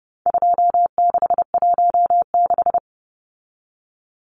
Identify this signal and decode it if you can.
Morse code, heard as a keyed tone, about 30 wpm, 700 Hz. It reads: 2616
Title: Numbers - 4-Digit 30wpm